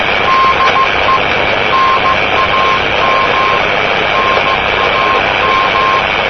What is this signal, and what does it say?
Morse code, heard as a keyed tone, about 16 wpm, 1060 Hz, with noise. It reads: DE NAM NAM
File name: namerica / usa / usa-3854-cw-ncmp-rb-1973